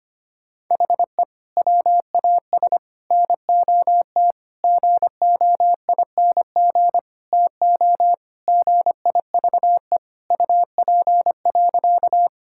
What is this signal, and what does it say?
Morse code, heard as a keyed tone, about 25 wpm, 700 Hz. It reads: HE WAS NOT GOING TO GIVE UP.
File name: practice / 25wpm / Sentences with 4-Word Phrases - 25wpm